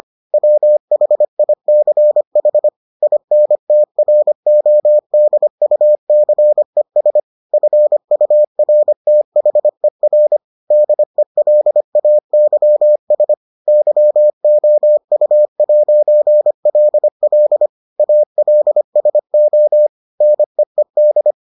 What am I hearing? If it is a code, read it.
Morse code, heard as a keyed tone, about 25 wpm, 600 Hz. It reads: WHICH INTRODUCES FURTHER DELAYS YOU'LL ALSO NEED